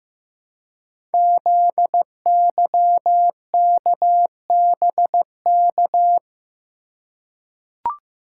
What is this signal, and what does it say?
Morse code, heard as a keyed tone, about 15 wpm, 700 Hz. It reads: ZYKBK
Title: Code Groups - 5 Letters - 15wpm